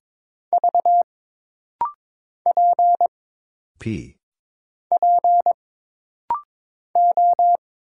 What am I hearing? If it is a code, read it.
Morse code, heard as a keyed tone, about 22 wpm, 700 Hz. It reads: V P P O
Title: Single Letter-Number Alt - Rapid-Fire 22wpm